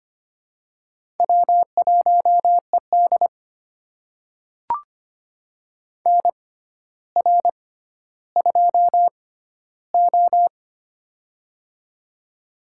Morse code, heard as a keyed tone, about 25 wpm, spaced as for 13 wpm, 700 Hz. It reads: W1ED NR2O
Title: USA Call Signs - 6x Character Spacing - 25wpm